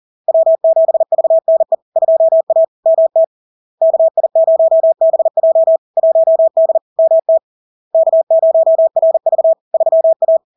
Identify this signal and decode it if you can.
Morse code, heard as a keyed tone, about 40 wpm, 650 Hz. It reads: W7VNE 2A MT KI0BJ 1D MT K0RV 3A
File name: LARK-FD-2022-Log-in-CW-40wpm-650-Hz